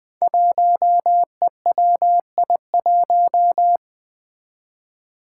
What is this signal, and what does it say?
Morse code, heard as a keyed tone, about 20 wpm, 700 Hz. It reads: 1EWI1
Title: Five Challenging Letter-Number Combinations 20wpm